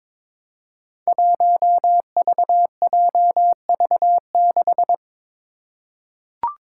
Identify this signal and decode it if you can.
Morse code, heard as a keyed tone, about 22 wpm, 700 Hz. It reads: 1VJV6
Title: Five Challenging Letter-Number Combinations 22wpm